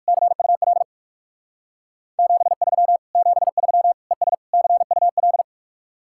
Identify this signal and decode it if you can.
Morse code, heard as a keyed tone, about 60 wpm, 700 Hz. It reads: CUL 73 73 ES CUL